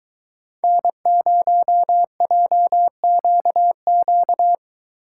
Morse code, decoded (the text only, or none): N0JQQ